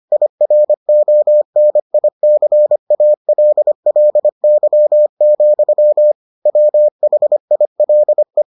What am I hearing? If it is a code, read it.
Morse code, heard as a keyed tone, about 25 wpm, 600 Hz. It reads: IRONICALLY, WHILE